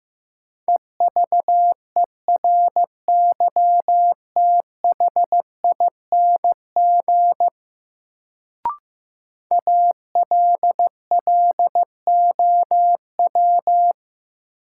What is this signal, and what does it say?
Morse code, heard as a keyed tone, about 16 wpm, 700 Hz. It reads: EVERYTHING ALLOW
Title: Sentences from Top 700 Words - New Word Drill - 15wpm